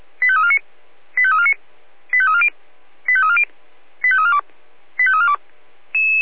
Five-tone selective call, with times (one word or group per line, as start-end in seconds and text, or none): 0.2-0.6 85349
1.2-1.5 85349
2.1-2.5 85340
3.1-3.4 85340
4.0-4.4 85342
5.0-5.4 85342